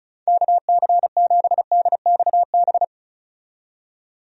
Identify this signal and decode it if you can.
Morse code, heard as a keyed tone, about 36 wpm, 700 Hz. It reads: KC7DXB
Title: USA Call Signs - 6 Characters - 35wpm